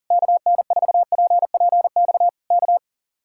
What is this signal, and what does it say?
Morse code, heard as a keyed tone, about 38 wpm, 700 Hz. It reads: KN4PPX K
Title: Speed-Racing Common QSO Phrases - 40wpm